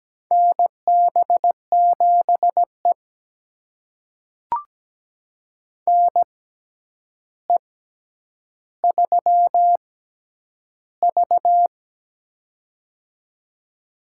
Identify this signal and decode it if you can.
Morse code, heard as a keyed tone, about 17 wpm, spaced as for 9 wpm, 700 Hz. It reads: NB7E NE3V
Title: USA Call Signs - 4 Characters - 6x Character Spacing - 17wpm